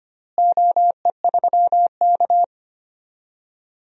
Morse code, heard as a keyed tone, about 25 wpm, 700 Hz. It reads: OE3K
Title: DXCC-100 Call Signs - 4 Characters - Repeated 3x - 25wpm